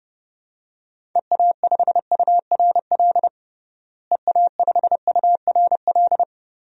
Morse code, heard as a keyed tone, about 32 wpm, 700 Hz. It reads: EA5URL EA5URL
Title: DXCC-100 Call Signs - 6 Characters - Repeated 2x - 30wpm